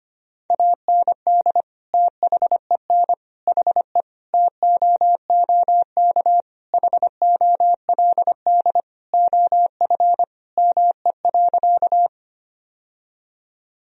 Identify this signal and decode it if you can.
Morse code, heard as a keyed tone, about 25 wpm, 700 Hz. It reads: AND THEN HE TOOK HOLD OF ME.
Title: Sentences from Top 1000 Words - Review Part 2 - 25wpm